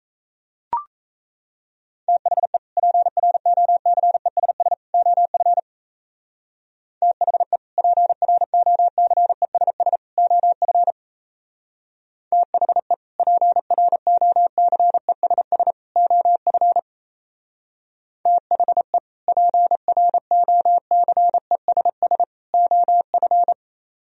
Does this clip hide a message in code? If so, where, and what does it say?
Yes: Morse code from 2.1 to 23.5 s, heard as a keyed tone, about 36 wpm, 700 Hz. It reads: THE PROCESS OF THE PROCESS OF THE PROCESS OF THE PROCESS OF